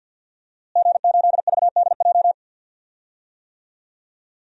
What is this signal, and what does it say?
Morse code, heard as a keyed tone, about 49 wpm, 700 Hz. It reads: G8VDJ
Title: DXCC-100 Call Signs - 5 Characters - Repeated 2x - 50wpm